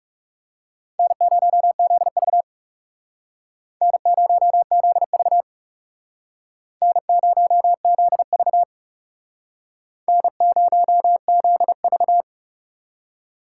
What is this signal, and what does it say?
Morse code, heard as a keyed tone, about 39 wpm, 700 Hz. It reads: N0ZV N0ZV N0ZV N0ZV